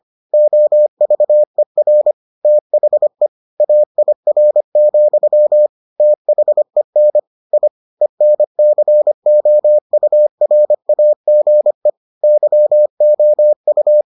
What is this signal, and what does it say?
Morse code, heard as a keyed tone, about 25 wpm, 600 Hz. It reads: OVER THE AIR, THEN I ENCOURAGE YOU